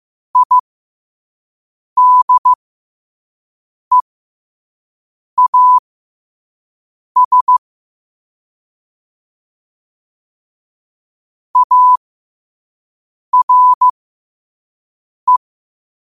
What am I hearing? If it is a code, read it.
Morse code, heard as a keyed tone, about 15 wpm, spaced as for 5 wpm, 1000 Hz. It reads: IDEAS ARE
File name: qotdfiles / 5wpm